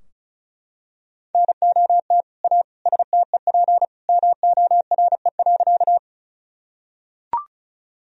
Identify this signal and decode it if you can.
Morse code, heard as a keyed tone, about 36 wpm, 700 Hz. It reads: NOT A STEP MORE.
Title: Sentences from Top 1000 Words - Review Part 2 - 35wpm